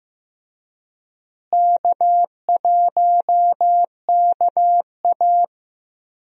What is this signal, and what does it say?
Morse code, heard as a keyed tone, about 15 wpm, 700 Hz. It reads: K1KA
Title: USA Call Signs - 4 Characters - Repeated 2x - 15wpm